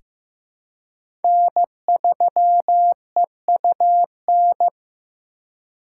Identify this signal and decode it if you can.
Morse code, heard as a keyed tone, about 15 wpm, 700 Hz. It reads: N3EUN
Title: USA Call Signs - 5 Characters - 3x Character Spacing - 15wpm